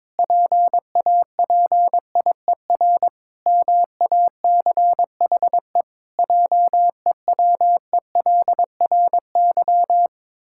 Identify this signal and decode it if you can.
Morse code, heard as a keyed tone, about 22 wpm, 700 Hz. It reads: PAPIER MACHE JEWELRY
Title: Pangram Challenge v2 - 22wpm